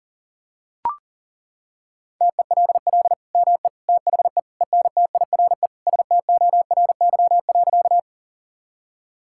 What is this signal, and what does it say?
Morse code, heard as a keyed tone, about 37 wpm, 700 Hz. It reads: TELL ME THE ENTIRE STORY.
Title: Sentences from Top 1000 Words - Review Part 5 - 40wpm